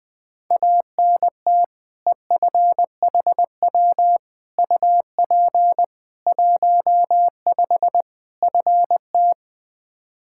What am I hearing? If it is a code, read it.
Morse code, heard as a keyed tone, about 21 wpm, 700 Hz. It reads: ANT EFHW UP 15 FT
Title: Practice QSO - Exchange 2 - Info - 20wpm